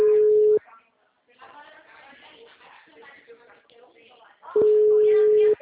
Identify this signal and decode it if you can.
Ringback tone (Europe, 425 Hz), with speech.
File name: out- -2021- - - .151